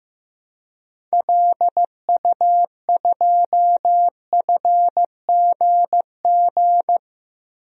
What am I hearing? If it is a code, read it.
Morse code, heard as a keyed tone, about 15 wpm, 700 Hz. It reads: LU2FGG
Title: DXCC-100 Call Signs - 6 Characters - Repeated 2x - 15wpm